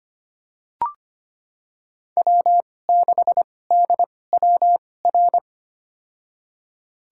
Morse code, heard as a keyed tone, about 25 wpm, spaced as for 18 wpm, 700 Hz. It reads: W6DWR
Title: USA Call Signs - 5 Characters - 2x Character Spacing - 25wpm